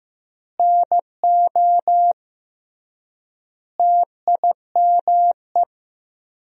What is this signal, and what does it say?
Morse code, heard as a keyed tone, about 15 wpm, 700 Hz. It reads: NO TIME
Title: 3x Word-Spacing - Sentences from Top 400 Words - 15wpm